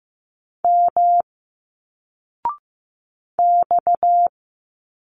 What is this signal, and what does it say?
Morse code, heard as a keyed tone, about 15 wpm, 700 Hz. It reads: M X
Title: Single Letters 15wpm